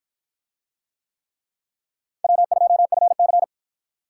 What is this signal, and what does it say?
Morse code, heard as a keyed tone, about 53 wpm, 700 Hz. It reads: W2FC